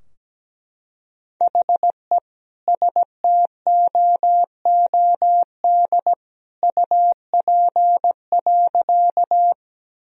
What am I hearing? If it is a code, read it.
Morse code, heard as a keyed tone, about 17 wpm, 700 Hz. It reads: HE STOOD UP.
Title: Sentences from Top 1000 Words - Review Part 2 - 17wpm